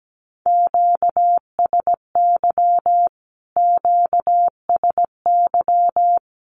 Morse code, heard as a keyed tone, about 17 wpm, 700 Hz. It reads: QSY QSY
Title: Q-Codes Repeated 3x 17wpm